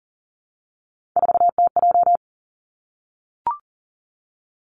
Morse code, heard as a keyed tone, about 40 wpm, 700 Hz. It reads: 4TJ
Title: Three Challenging Letter-Number Combinations 40wpm